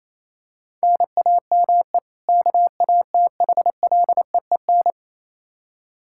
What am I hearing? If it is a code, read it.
Morse code, heard as a keyed tone, about 27 wpm, 700 Hz. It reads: NAME KATHLEEN
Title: Practice QSO - Exchange 1 - Beginning - 28wpm